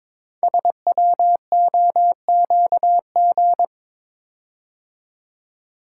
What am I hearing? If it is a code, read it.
Morse code, heard as a keyed tone, about 22 wpm, 690 Hz. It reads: SWOQG
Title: Code Groups - 5 Letters - 22wpm